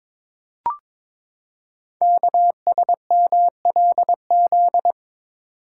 Morse code, heard as a keyed tone, about 22 wpm, 700 Hz. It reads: KSMLZ
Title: Code Groups - 5 Letters - 22wpm